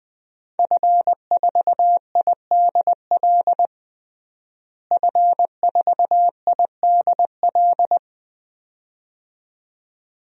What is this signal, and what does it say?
Morse code, heard as a keyed tone, about 20 wpm, 700 Hz. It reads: F4IDL F4IDL
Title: DXCC-100 Call Signs - 5 Characters - Repeated 2x - 20wpm